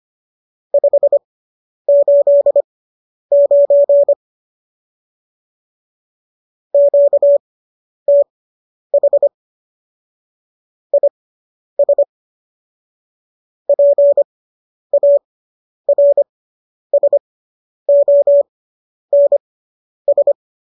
Morse code, heard as a keyed tone, about 25 wpm, spaced as for 10 wpm, 570 Hz. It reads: 589 QTH IS PARSONS